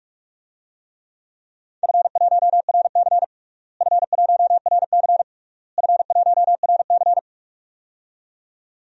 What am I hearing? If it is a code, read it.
Morse code, heard as a keyed tone, about 41 wpm, 700 Hz. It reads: F1RC F1RC F1RC
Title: DXCC-100 Call Signs - 4 Characters - Repeated 3x - 45wpm